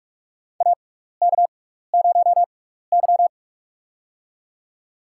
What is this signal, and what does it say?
Morse code, heard as a keyed tone, about 46 wpm, spaced as for 16 wpm, 700 Hz. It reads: AK0Y
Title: USA Call Signs - 6x Character Spacing - 45wpm